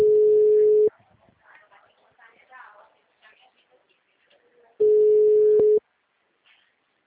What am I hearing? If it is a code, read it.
Ringback tone (Europe, 425 Hz), with speech.